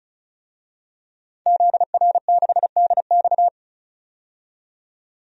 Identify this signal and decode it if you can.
Morse code, heard as a keyed tone, about 36 wpm, 700 Hz. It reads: ZR6DX
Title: DXCC-100 Call Signs - 5 Characters - 35wpm